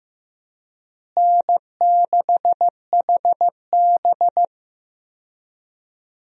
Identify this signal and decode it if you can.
Morse code, heard as a keyed tone, about 15 wpm, 700 Hz. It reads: N6HB
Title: USA Call Signs - 4 Characters - Repeated 2x - 15wpm